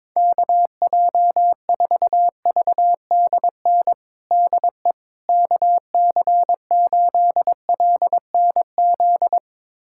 Morse code, heard as a keyed tone, about 22 wpm, 700 Hz. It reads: KJ4VDN DE KC8LNZ
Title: USA Callsign DE Callsign - 6 Characters - 22wpm